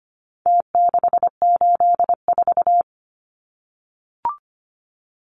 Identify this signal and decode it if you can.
Morse code, heard as a keyed tone, about 25 wpm, 700 Hz. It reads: T684
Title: Numbers - Contest Style - 4-Digit 25wpm